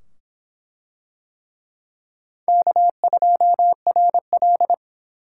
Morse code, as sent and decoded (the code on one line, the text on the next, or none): -.- ..--- .-. .-..
K2RL